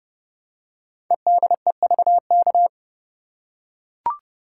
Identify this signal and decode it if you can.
Morse code, heard as a keyed tone, about 30 wpm, 690 Hz. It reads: EDEVK
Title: Code Groups - 5 Letters - 30wpm